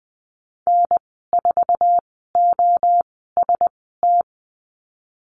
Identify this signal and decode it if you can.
Morse code, heard as a keyed tone, about 20 wpm, spaced as for 15 wpm, 700 Hz. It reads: N4OST
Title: 088 Lesson 15g - Number 5 - Call Signs 20wpm